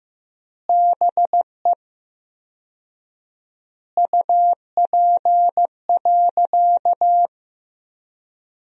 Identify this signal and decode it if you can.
Morse code, heard as a keyed tone, about 15 wpm, 700 Hz. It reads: BE UP.
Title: 4x Word-Spacing - Sentences from Top 300 Words - 15wpm